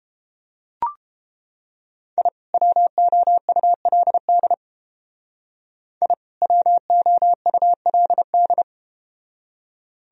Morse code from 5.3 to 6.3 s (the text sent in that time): I